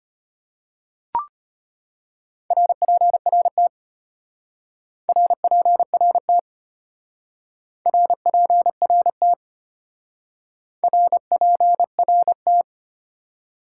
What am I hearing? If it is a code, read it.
Morse code, heard as a keyed tone, about 30 wpm, 700 Hz. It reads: RPRT RPRT RPRT RPRT
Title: Speed-Racing Top 90 QSO Elements 25wpm